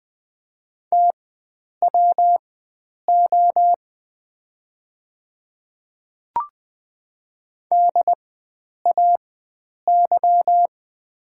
Morse code, heard as a keyed tone, about 20 wpm, spaced as for 9 wpm, 700 Hz. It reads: TWO DAY
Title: 4x Character Spacing - Top 100 Words - 20wpm